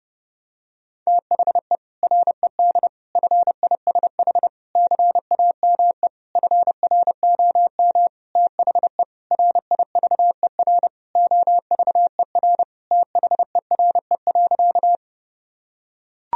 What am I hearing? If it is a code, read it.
Morse code, heard as a keyed tone, about 30 wpm, 700 Hz. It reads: THE RED FISH CAME FROM THE RIVER OVER THERE.
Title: Sentences from Top 1000 Words - Review Part 2 - 30wpm